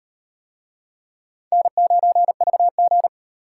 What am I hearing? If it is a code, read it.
Morse code, heard as a keyed tone, about 38 wpm, 700 Hz. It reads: N9VG